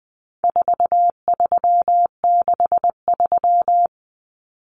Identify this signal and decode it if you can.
Morse code, heard as a keyed tone, about 20 wpm, 700 Hz. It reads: 4363
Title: Common Cut Numbers - 4-Digit 20wpm